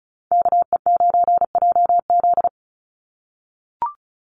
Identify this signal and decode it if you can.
Morse code, heard as a keyed tone, about 34 wpm, 700 Hz. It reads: KE9JZ
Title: USA Call Signs 35wpm